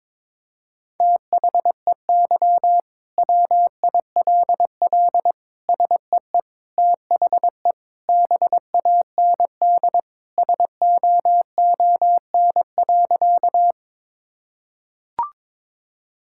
Morse code, heard as a keyed tone, about 22 wpm, 700 Hz. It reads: THEY WILL SEE THE BAND SOON.
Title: Sentences from Top 1600 Words 22wpm